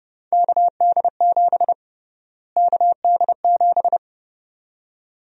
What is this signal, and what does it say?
Morse code, heard as a keyed tone, about 30 wpm, 700 Hz. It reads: KD7 KD7
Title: USA Call Sign Prefixes - Repeated 2x - 30wpm